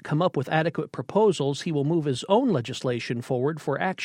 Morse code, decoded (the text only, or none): none